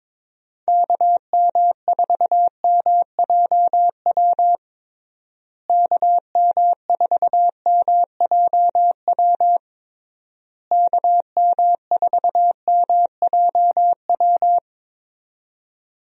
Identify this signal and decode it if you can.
Morse code, heard as a keyed tone, about 22 wpm, 700 Hz. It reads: KM4MJW KM4MJW KM4MJW
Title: USA Call Signs - 6 Characters - Repeated 3x - 22wpm